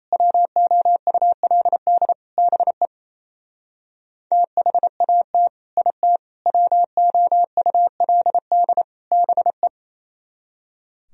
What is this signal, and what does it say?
Morse code, heard as a keyed tone, about 36 wpm, 700 Hz. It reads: WOULD BE THAT IT WOULD BE